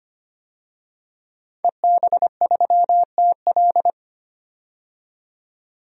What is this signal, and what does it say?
Morse code, heard as a keyed tone, about 25 wpm, 700 Hz. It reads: EB3TL
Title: DXCC-100 Call Signs - 5 Characters - 25wpm